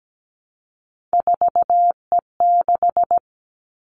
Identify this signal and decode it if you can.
Morse code, heard as a keyed tone, about 17 wpm, 700 Hz. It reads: 4E6
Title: Three Challenging Letter-Number Combinations 17wpm